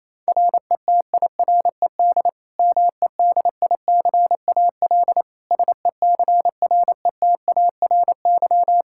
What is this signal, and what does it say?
Morse code, heard as a keyed tone, about 28 wpm, 700 Hz. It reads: RETIRED MEDICAL SECRETARY
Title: Practice QSO - Exchange 3 - More Info - 28wpm